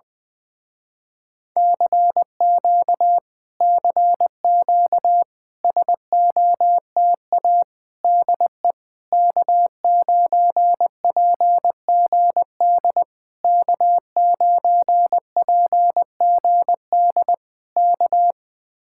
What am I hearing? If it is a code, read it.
Morse code, heard as a keyed tone, about 20 wpm, 700 Hz. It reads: CQ CQ SOTA DE K9PGD K9PGD K
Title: Speed-Racing Common QSO Phrases Encore - 17wpm